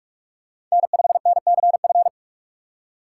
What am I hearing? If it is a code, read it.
Morse code, heard as a keyed tone, about 44 wpm, 700 Hz. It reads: N5NCF
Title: USA Call Signs - 5 Characters - 6x Character Spacing - 45wpm